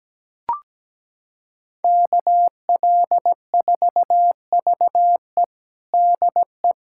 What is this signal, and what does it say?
Morse code, heard as a keyed tone, about 17 wpm, 700 Hz. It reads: KL4VE DE
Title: USA Callsign DE Callsign - 5 Characters - 17wpm